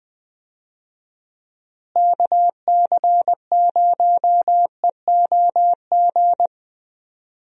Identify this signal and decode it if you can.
Morse code, heard as a keyed tone, about 20 wpm, 700 Hz. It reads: KC0EOG
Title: USA Call Signs - 6 Characters - Repeated 3x - 20wpm